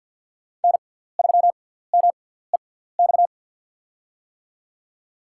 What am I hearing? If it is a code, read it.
Morse code, heard as a keyed tone, about 49 wpm, spaced as for 17 wpm, 700 Hz. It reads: N3MEX
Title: USA Call Signs - 5 Characters - 6x Character Spacing - 50wpm